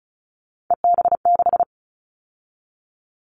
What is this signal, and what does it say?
Morse code, heard as a keyed tone, about 36 wpm, 700 Hz. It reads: EB6
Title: Callsign Prefixes for Top 50 DXCC - 35wpm